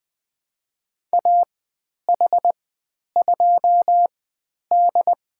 Morse code, heard as a keyed tone, about 20 wpm, spaced as for 10 wpm, 700 Hz. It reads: AH2D
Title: 095 Lesson 16g - Letter c - Call Signs 20wpm